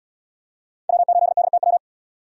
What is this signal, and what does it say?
Morse code, heard as a keyed tone, about 77 wpm, 700 Hz. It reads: W8DTW